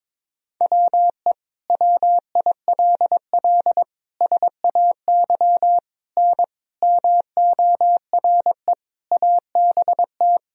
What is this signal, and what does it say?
Morse code, heard as a keyed tone, about 22 wpm, 700 Hz. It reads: WE WILL SAY N MORE ABT
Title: Sentences from Top 200 Words and Abbreviations 22wpm